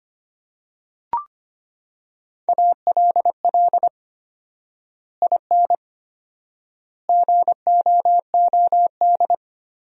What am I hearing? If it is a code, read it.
Morse code, heard as a keyed tone, about 25 wpm, 700 Hz. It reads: ALL IN GOOD
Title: 4x Word-Spacing - Sentences from Top 200 Words - 25wpm